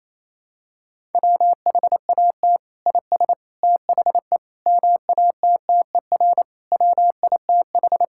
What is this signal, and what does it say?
Morse code, heard as a keyed tone, about 28 wpm, 700 Hz. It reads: WHAT IS THE MATTER WITH
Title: Sentences from Top 1000 Words - Review Part 5 - 28wpm